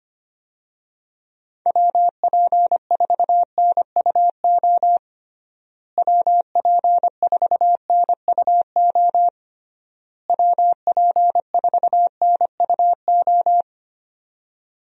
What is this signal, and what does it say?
Morse code, heard as a keyed tone, about 25 wpm, 700 Hz. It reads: WP4NUO WP4NUO WP4NUO
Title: USA Call Signs - 6 Characters - Repeated 3x - 25wpm